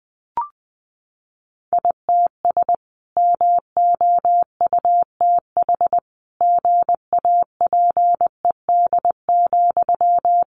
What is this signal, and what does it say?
Morse code, heard as a keyed tone, about 20 wpm, 700 Hz. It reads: ITS MOUTH GAPED,
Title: A Sound Of Thunder 20wpm